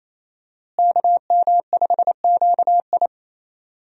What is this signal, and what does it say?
Morse code, heard as a keyed tone, about 27 wpm, 700 Hz. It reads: KM5QI
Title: USA Call Signs - 5 Characters - Repeated 3x - 28wpm